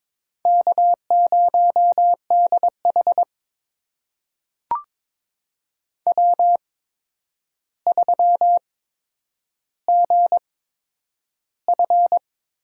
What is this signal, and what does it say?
Morse code, heard as a keyed tone, about 22 wpm, spaced as for 9 wpm, 700 Hz. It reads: K0DH W3GF
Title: USA Call Signs - 4 Characters - 8x Character Spacing - 22wpm